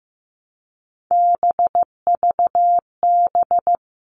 Morse code, heard as a keyed tone, about 15 wpm, 700 Hz. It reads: BVB
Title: All Cut Numbers - 3-Digit 15wpm